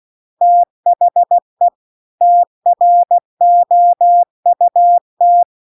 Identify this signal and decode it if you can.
Morse code, heard as a keyed tone, about 16 wpm, 700 Hz. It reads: THE TROUT
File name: CW-The-Big-Twoheaded-River-2-16WPM